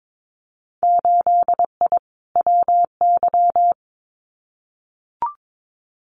Morse code, heard as a keyed tone, about 22 wpm, 700 Hz. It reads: 8I WY
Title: Winter Field Day Exchanges Encore 22wpm